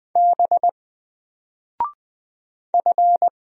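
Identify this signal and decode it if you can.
Morse code, heard as a keyed tone, about 20 wpm, 700 Hz. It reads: B F
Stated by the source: Single letter and number Morse drill at 20 wpm, from N4LCD.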